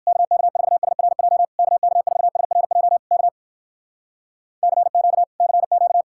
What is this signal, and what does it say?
Morse code, heard as a keyed tone, about 56 wpm, 700 Hz. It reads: KK4SRJ KK4SRJ K CQ CQ